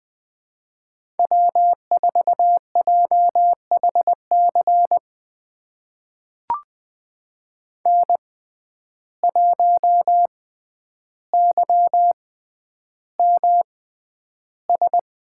Morse code, heard as a keyed tone, about 20 wpm, spaced as for 10 wpm, 700 Hz. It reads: W4JHC N1YMS
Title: USA Call Signs - 5 Characters - 6x Character Spacing - 20wpm